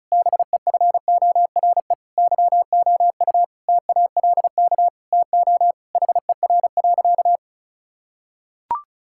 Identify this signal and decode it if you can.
Morse code, heard as a keyed tone, about 36 wpm, 690 Hz. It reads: BEFORE YOU TALK TO HER.
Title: Sentences from Top 1000 Words - Review Part 6 - 35wpm